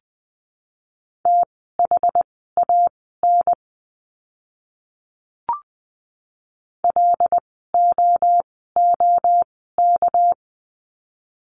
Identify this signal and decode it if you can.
Morse code, heard as a keyed tone, about 20 wpm, spaced as for 14 wpm, 700 Hz. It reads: THAN LOOK
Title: 237 Lesson 38f - Letter q - Words 20wpm